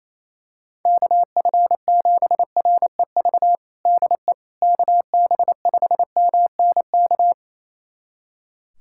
Morse code, heard as a keyed tone, about 28 wpm, 700 Hz. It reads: KF7REV DE KB5MNK
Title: USA Callsign DE Callsign - 6 Characters - 28wpm